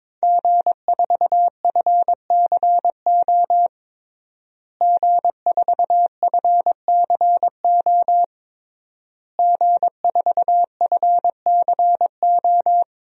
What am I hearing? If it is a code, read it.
Morse code, heard as a keyed tone, about 22 wpm, 700 Hz. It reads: G4FCO G4FCO G4FCO
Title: DXCC-100 Call Signs - 5 Characters - Repeated 3x - 22wpm